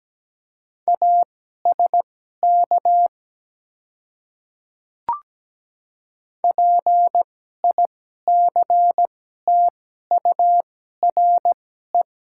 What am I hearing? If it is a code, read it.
Morse code, heard as a keyed tone, about 17 wpm, spaced as for 12 wpm, 700 Hz. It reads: ASK PICTURE
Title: 2x Character Spacing - Top 200 Words - 17wpm